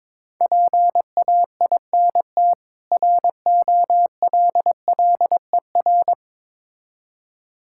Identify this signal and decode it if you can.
Morse code, heard as a keyed tone, about 22 wpm, 700 Hz. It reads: PAINT ROLLER
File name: Difficult Words with Hints - 22wpm